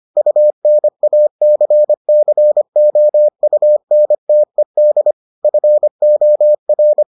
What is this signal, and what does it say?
Morse code, heard as a keyed tone, about 25 wpm, 600 Hz. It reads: UNACCOUNTED FOR